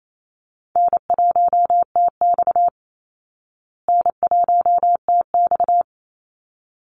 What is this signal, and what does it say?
Morse code, heard as a keyed tone, about 28 wpm, 700 Hz. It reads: N1TX N1TX
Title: Easy Call Signs Repeated 2x - 28wpm